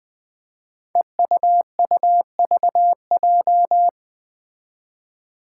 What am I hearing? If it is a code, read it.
Morse code, heard as a keyed tone, about 20 wpm, 700 Hz. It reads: EUUVJ
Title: Five Challenging Letter-Number Combinations 20wpm